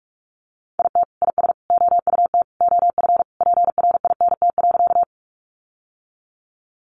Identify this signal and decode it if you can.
Morse code, heard as a keyed tone, about 46 wpm, 700 Hz. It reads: IT IS OUT OF PRINT.
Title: Sentences from Top 1400 Words 45wpm